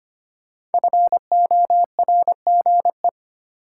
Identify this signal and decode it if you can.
Morse code, heard as a keyed tone, about 25 wpm, 700 Hz. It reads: FORGE
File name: 5 Letter Words - First 3 Letters Given - 25wpm